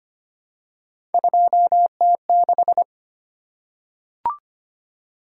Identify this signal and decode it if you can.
Morse code, heard as a keyed tone, about 25 wpm, 700 Hz. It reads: 2T6
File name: Three Letter-Number Combinations - 25wpm